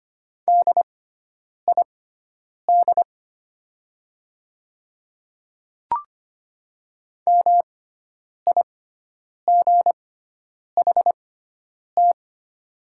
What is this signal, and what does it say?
Morse code, heard as a keyed tone, about 25 wpm, spaced as for 9 wpm, 700 Hz. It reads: DID MIGHT
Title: 6x Character Spacing - Top 500 Words - 25wpm